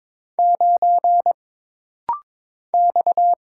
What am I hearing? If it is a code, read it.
Morse code, heard as a keyed tone, about 22 wpm, 700 Hz. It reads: 9 X
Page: This rapid-fire timing is geared to push your ICR (Instant Character Recognition) skills.